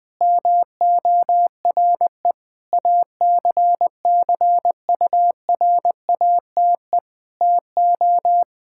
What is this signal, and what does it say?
Morse code, heard as a keyed tone, about 21 wpm, 700 Hz. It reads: MORE ACCURATE TO
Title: Sentences with 8-Word Phrases - 20wpm